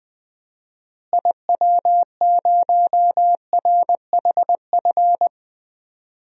Morse code, heard as a keyed tone, about 21 wpm, 700 Hz. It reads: IW0RHF
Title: DXCC-100 Call Signs - 6 Characters - Repeated 3x - 20wpm